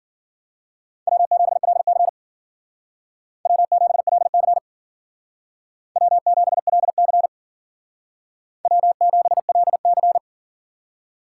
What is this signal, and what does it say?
Morse code, heard as a keyed tone, about 50 wpm, 700 Hz. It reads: W7LC W7LC W7LC W7LC
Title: Speed Racing - USA 1x2 Call Signs 40wpm